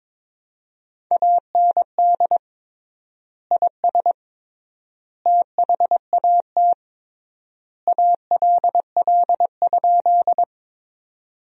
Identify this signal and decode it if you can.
Morse code, heard as a keyed tone, about 22 wpm, 700 Hz. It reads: AND IS THAT ALL?
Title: 3x Word-Spacing - Sentences from Top 100 Words - 22wpm